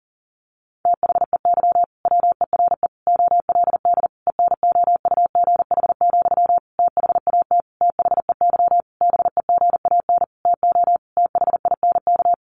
Morse code, heard as a keyed tone, about 40 wpm, 700 Hz. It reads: THEY WERE OLD ENOUGH, THAT THEY BEGAN TO THINK
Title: Sentences from Top 800 Words 40wpm